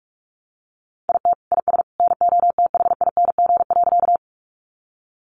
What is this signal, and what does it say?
Morse code, heard as a keyed tone, about 44 wpm, 700 Hz. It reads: IT IS NOTHING.
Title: Sentences from Top 500 Words 45wpm